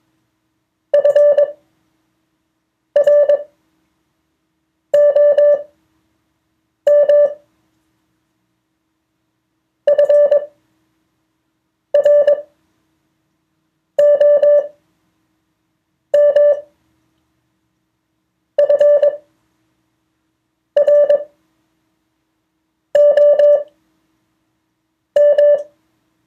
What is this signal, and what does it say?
Morse code, heard as a keyed tone, about 21 wpm, spaced as for 5 wpm, 580 Hz. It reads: FROM FROM FROM
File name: 2of4-words-1